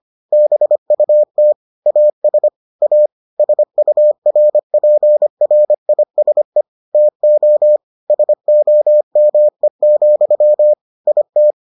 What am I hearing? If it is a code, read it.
Morse code, heard as a keyed tone, about 25 wpm, 600 Hz. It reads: BUT AS A SURPRISE TO SOME, IT